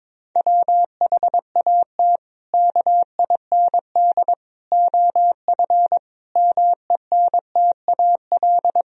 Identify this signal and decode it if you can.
Morse code, heard as a keyed tone, about 22 wpm, 700 Hz. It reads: WHAT KIND OF MENTAL
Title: Sentences from Top 1000 Words - Review Part 4 - 22wpm